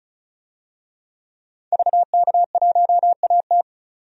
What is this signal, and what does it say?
Morse code, heard as a keyed tone, about 36 wpm, 700 Hz. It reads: VK1AT
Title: DXCC-100 Call Signs - 5 Characters - 35wpm